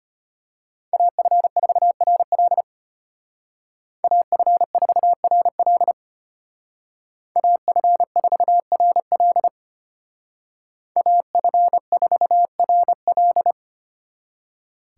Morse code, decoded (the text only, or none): AF4RL AF4RL AF4RL AF4RL